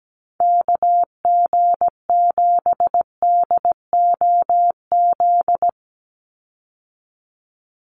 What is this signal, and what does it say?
Morse code, heard as a keyed tone, about 17 wpm, 700 Hz. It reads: KG7DOZ
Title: USA Call Signs 17wpm